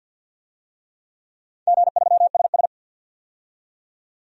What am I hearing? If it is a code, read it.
Morse code, heard as a keyed tone, about 45 wpm, 700 Hz. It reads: G3SS
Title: DXCC-100 Call Signs - 4 Characters - 50wpm